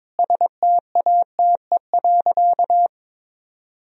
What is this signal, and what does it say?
Morse code, heard as a keyed tone, about 21 wpm, 700 Hz. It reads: STATE.
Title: Sentences from Top 1000 Words - Review Part 2 - 22wpm